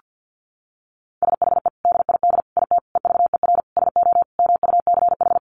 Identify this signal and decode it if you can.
Morse code, heard as a keyed tone, about 49 wpm, 700 Hz. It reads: SHE DID IT EVER SO MUCH
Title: Sentences from Top 400 Words 50wpm